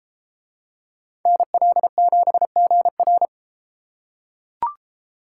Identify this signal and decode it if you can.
Morse code, heard as a keyed tone, about 31 wpm, 700 Hz. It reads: NL7GR